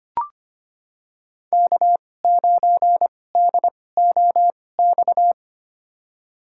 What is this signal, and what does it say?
Morse code, heard as a keyed tone, about 25 wpm, spaced as for 18 wpm, 700 Hz. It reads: K9DOX
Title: USA Call Signs - 5 Characters - 2x Character Spacing - 25wpm